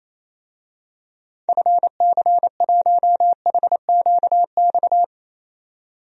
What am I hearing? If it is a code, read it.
Morse code, heard as a keyed tone, about 28 wpm, 700 Hz. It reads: FC1HQX
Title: DXCC-100 Call Signs - 6 Characters - 28wpm